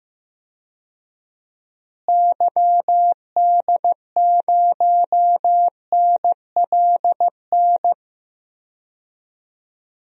Morse code, heard as a keyed tone, about 15 wpm, 700 Hz. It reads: YD0NLN
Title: DXCC-100 Call Signs - 6 Characters - Repeated 3x - 15wpm